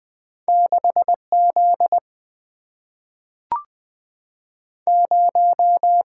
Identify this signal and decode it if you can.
Morse code, heard as a keyed tone, about 20 wpm, 700 Hz. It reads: 6Z 0
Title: Single-Letter Single-Number 20wpm